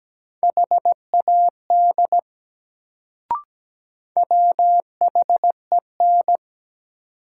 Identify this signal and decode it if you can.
Morse code, heard as a keyed tone, about 17 wpm, 700 Hz. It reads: HAD WHEN
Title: Sets of 2 Words - New Word Drill - 17wpm